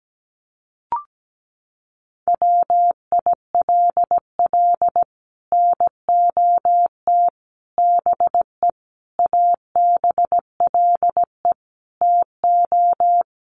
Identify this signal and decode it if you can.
Morse code, heard as a keyed tone, about 17 wpm, 700 Hz. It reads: WILL NOT BE ABLE TO
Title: Sets of 5 Words 17wpm